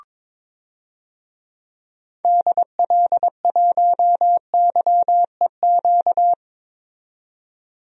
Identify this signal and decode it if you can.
Morse code, heard as a keyed tone, about 22 wpm, 700 Hz. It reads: DL1YEQ
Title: DXCC-100 Call Signs - 6 Characters - 22wpm